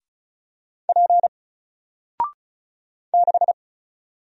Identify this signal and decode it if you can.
Morse code, heard as a keyed tone, about 36 wpm, 700 Hz. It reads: P 6
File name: Single-Letter-Number-35wpm